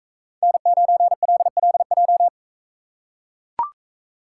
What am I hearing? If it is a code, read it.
Morse code, heard as a keyed tone, about 43 wpm, 700 Hz. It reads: N9LLJ